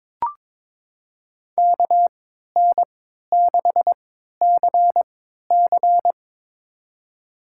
Morse code, heard as a keyed tone, about 22 wpm, spaced as for 13 wpm, 700 Hz. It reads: KN6CC
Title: USA Call Signs - 5 Characters - 3x Character Spacing - 22wpm